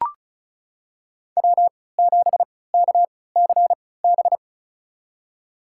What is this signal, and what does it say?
Morse code, heard as a keyed tone, about 36 wpm, spaced as for 20 wpm, 700 Hz. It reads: W7KCB
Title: USA Call Signs - 3x Character Spacing - 35wpm